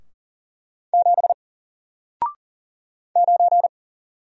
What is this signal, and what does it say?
Morse code, heard as a keyed tone, about 40 wpm, 700 Hz. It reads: 7 9
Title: Single Character (Letter:Number:Punctuation:Prosign) Alt 40wpm